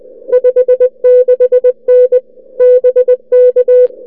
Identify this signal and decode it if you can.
Morse code, heard as a keyed tone, about 21 wpm, 500 Hz. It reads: 56N BK
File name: n6tr / n6tr qso sm0mdg @ oz1lxj 0230utc 16dec2008